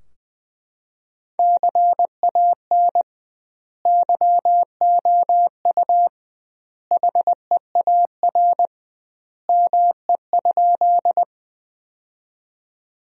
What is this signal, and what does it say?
Morse code, heard as a keyed tone, about 21 wpm, 700 Hz. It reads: CAN YOU HEAR ME?
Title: 2x Word-Spacing - Sentences from Top 300 Words - 20wpm